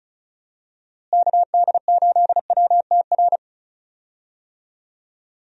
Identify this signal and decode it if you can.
Morse code, heard as a keyed tone, about 34 wpm, 700 Hz. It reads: KD8WTR
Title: USA Call Signs - 6 Characters - Repeated 3x - 35wpm